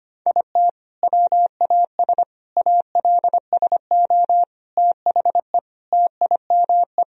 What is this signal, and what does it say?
Morse code, heard as a keyed tone, about 25 wpm, 700 Hz. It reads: IT WAS ALSO THE TIME
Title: Sentences from Top 1000 Words - Review Part 1 - 25wpm